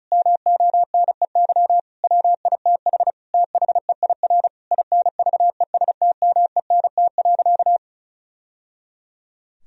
Morse code, heard as a keyed tone, about 34 wpm, 700 Hz. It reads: MONEY WITH THEIR INVESTMENT.
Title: Sentences from Top 1000 Words - Review Part 4 - 35wpm